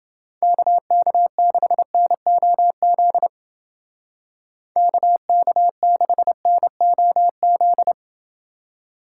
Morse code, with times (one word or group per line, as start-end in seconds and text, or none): 0.4-3.3 KK6NOZ
4.8-7.9 KK6NOZ